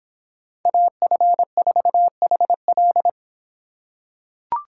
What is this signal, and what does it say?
Morse code, heard as a keyed tone, about 26 wpm, 700 Hz. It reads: AF4HL